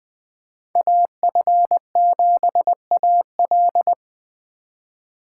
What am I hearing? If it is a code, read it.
Morse code, heard as a keyed tone, about 20 wpm, 700 Hz. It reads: AF7AL
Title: Speed Racing - USA 2x2 Call Signs 17wpm